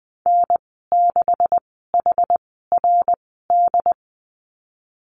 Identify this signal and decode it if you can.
Morse code, heard as a keyed tone, about 19 wpm, spaced as for 14 wpm, 700 Hz. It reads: N6HRD
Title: 154 Lesson 25g - Letter p - Call Signs 20wpm